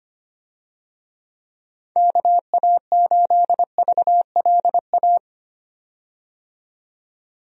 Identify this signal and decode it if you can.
Morse code, heard as a keyed tone, about 25 wpm, 700 Hz. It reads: KA8VLA
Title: USA Call Signs - 6 Characters - 25wpm